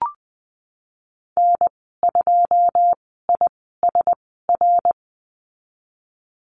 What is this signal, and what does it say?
Morse code, heard as a keyed tone, about 20 wpm, spaced as for 15 wpm, 700 Hz. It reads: N2ISR
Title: 082 Lesson 14g - Number 2 - Call Signs 20wpm